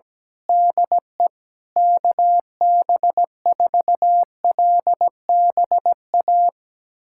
Morse code, heard as a keyed tone, about 17 wpm, 700 Hz. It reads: DE KB4LBA
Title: USA Callsign DE Callsign 17wpm